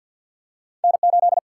Morse code, heard as a keyed tone, about 54 wpm, 700 Hz. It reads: N8
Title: Single-Letter Single-Number 50wpm